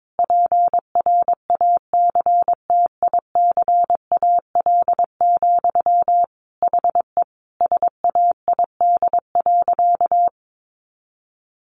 Morse code, heard as a keyed tone, about 22 wpm, 700 Hz. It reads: PRACTICAL, HE SAID.
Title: Sentences from Top 1400 Words 22wpm